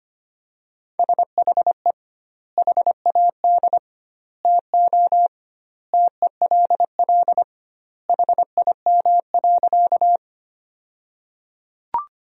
Morse code, heard as a keyed tone, about 25 wpm, 700 Hz. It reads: SHE HAD TO TELL HIM.
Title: 2x Word-Spacing - Sentences from Top 200 Words - 25wpm